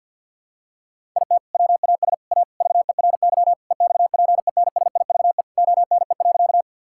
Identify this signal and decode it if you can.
Morse code, heard as a keyed tone, about 53 wpm, 700 Hz. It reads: IT WAS A VERY EXPENSIVE ONE.
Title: Sentences from Top 1000 Words - Review Part 6 - 50wpm